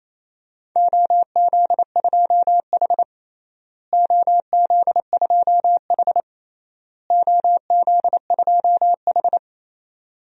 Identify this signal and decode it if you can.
Morse code, heard as a keyed tone, about 28 wpm, 700 Hz. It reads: OZ2H OZ2H OZ2H
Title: DXCC-100 Call Signs - 4 Characters - Repeated 3x - 28wpm